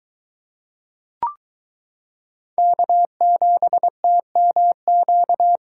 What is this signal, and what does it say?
Morse code, heard as a keyed tone, about 22 wpm, 700 Hz. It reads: K7TMQ